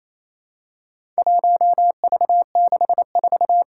Morse code, heard as a keyed tone, about 28 wpm, 700 Hz. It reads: 1V64
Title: Four Challenging Letter-Number Combinations 28wpm